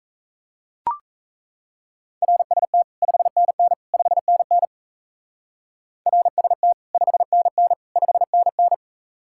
Morse code, heard as a keyed tone, about 34 wpm, 700 Hz. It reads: RST 5NN 5NN RST 5NN 5NN
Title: Speed-Racing Common QSO Phrases - 28wpm